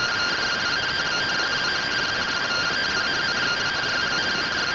Two-tone teleprinter signal, about 170 Hz shift.